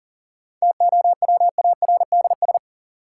Morse code, heard as a keyed tone, about 40 wpm, 700 Hz. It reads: TOWARDS
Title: Sentences from Top 1000 Words - Review Part 2 - New Word Drill - 40wpm